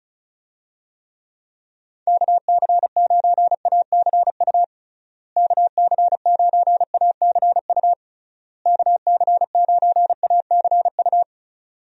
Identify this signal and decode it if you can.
Morse code, heard as a keyed tone, about 36 wpm, 700 Hz. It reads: KC9ACU KC9ACU KC9ACU
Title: USA Call Signs - 6 Characters - Repeated 3x - 35wpm